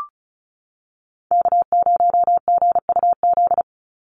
Morse code, heard as a keyed tone, about 34 wpm, 700 Hz. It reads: K0GUZ
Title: Call Signs Repeated 3x - 35wpm